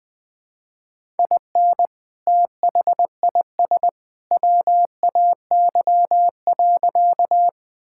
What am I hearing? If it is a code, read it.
Morse code, heard as a keyed tone, about 20 wpm, 700 Hz. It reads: IN THIS WAY.
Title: Sentences with 2-Word Phrases - Spoken Beginning - 20wpm